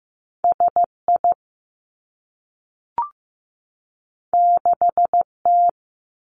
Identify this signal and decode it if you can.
Morse code, heard as a keyed tone, about 15 wpm, 700 Hz. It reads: S I 6 T
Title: Two Challenging Letter-Number Combinations 15wpm